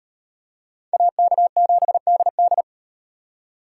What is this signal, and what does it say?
Morse code, heard as a keyed tone, about 33 wpm, 700 Hz. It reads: AK7DD